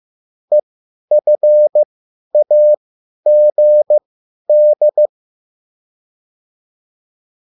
Morse code, heard as a keyed tone, about 15 wpm, spaced as for 10 wpm, 600 Hz. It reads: EFAGD
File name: QSL / CW lesson